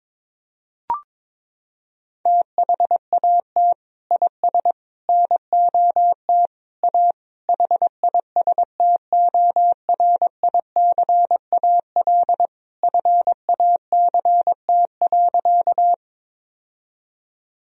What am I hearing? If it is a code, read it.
Morse code, heard as a keyed tone, about 22 wpm, 700 Hz. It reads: THAT IS NOT A HISTORICAL FACT.
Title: Sentences from Top 1000 Words - Review Part 4 - 22wpm